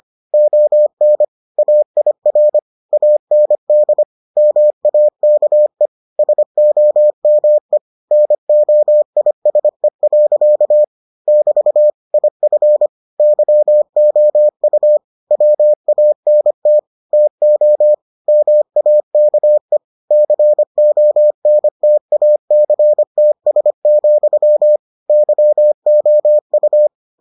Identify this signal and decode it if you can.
Morse code, heard as a keyed tone, about 25 wpm, 600 Hz. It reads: ON AIR AND MAKE SOME NOISE. <BT> IF YOU WANT TO MAKE CONTACTS, YOU